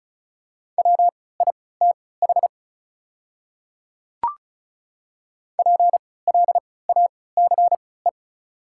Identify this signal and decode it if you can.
Morse code, heard as a keyed tone, about 36 wpm, spaced as for 20 wpm, 690 Hz. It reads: WITH PLACE
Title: 3x Character Spacing - Top 200 Words - 35wpm